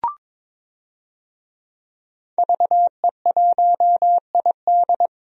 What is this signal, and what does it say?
Morse code, heard as a keyed tone, about 22 wpm, 700 Hz. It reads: VE1ID
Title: DXCC-100 Call Signs - 5 Characters - 22wpm